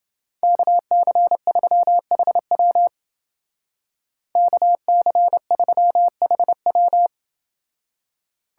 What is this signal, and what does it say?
Morse code, heard as a keyed tone, about 28 wpm, 700 Hz. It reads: KC3HW KC3HW